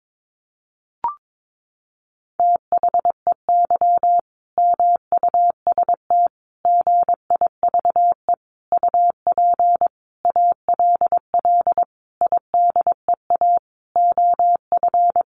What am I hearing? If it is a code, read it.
Morse code, heard as a keyed tone, about 22 wpm, 700 Hz. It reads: THEY MUST GIVE UP ALL IDEA OF
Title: Sentences from Top 1100 Words 22wpm